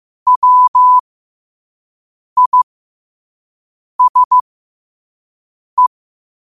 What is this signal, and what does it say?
Morse code, heard as a keyed tone, about 15 wpm, spaced as for 5 wpm, 1000 Hz. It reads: WISE
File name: qotdfiles / 5wpm